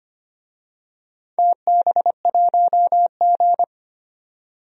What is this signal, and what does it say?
Morse code, heard as a keyed tone, about 25 wpm, 700 Hz. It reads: TB1G
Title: DXCC-100 Call Signs - 4 Characters - 25wpm